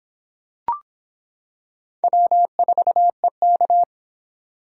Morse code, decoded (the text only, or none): W4EK